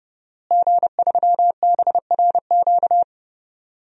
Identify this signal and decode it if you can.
Morse code, heard as a keyed tone, about 30 wpm, 700 Hz. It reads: G3BRQ
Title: DXCC-100 Call Signs - 5 Characters - Repeated 2x - 30wpm